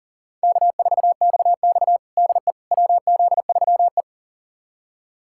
Morse code, heard as a keyed tone, about 40 wpm, 700 Hz. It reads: K4XX DE WZ3E
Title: USA Callsign DE Callsign - 4 Characters - 40wpm